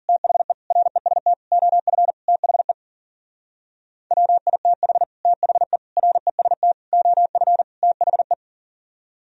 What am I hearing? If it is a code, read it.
Morse code, heard as a keyed tone, about 52 wpm, 700 Hz. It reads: THE REST OF THE WITH THE REST OF THE